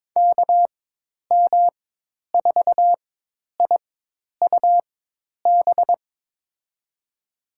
Morse code, heard as a keyed tone, about 22 wpm, spaced as for 10 wpm, 700 Hz. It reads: KM4IUB
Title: USA Call Signs - 6 Characters - 4x Character Spacing - 22wpm